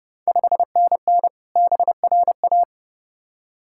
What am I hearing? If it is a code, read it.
Morse code, heard as a keyed tone, about 30 wpm, 700 Hz. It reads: 5NN BRA
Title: Ontario QSO Party Exchange 30wpm